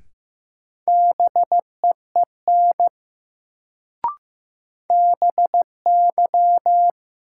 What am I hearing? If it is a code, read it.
Morse code, heard as a keyed tone, about 15 wpm, 700 Hz. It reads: BEEN BY
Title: Sentences from Top 100 Words - New Word Drill - 15wpm